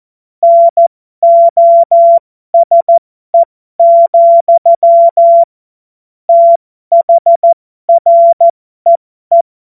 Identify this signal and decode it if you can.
Morse code, heard as a keyed tone, about 14 wpm, spaced as for 12 wpm, 680 Hz. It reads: NOSE, THREE
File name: Morsekurs / Witze